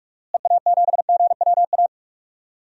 Morse code, heard as a keyed tone, about 43 wpm, 700 Hz. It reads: EA7GWA
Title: DXCC-100 Call Signs - 6 Characters - Repeated 3x - 45wpm